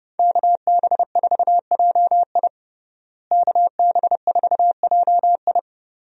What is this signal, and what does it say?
Morse code, heard as a keyed tone, about 32 wpm, 700 Hz. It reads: KB4JI KB4JI
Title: USA Call Signs - 5 Characters - Repeated 2x - 30wpm